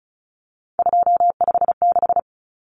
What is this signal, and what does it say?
Morse code, heard as a keyed tone, about 34 wpm, 700 Hz. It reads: 256
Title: Common Cut Numbers - 3-Digit 35wpm